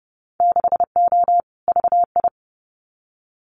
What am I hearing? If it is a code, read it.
Morse code, heard as a keyed tone, about 30 wpm, 700 Hz. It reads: 6O VI
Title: Winter Field Day Exchanges Encore 30wpm